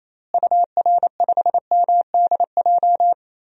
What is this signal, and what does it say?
Morse code, heard as a keyed tone, about 28 wpm, 700 Hz. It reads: UR5MDJ
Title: DXCC-100 Call Signs - 6 Characters - Repeated 3x - 28wpm